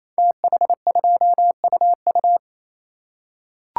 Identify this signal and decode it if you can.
Morse code, heard as a keyed tone, about 29 wpm, 700 Hz. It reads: TH2UU
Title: Code Groups - 5 Letter:Numbers - 28wpm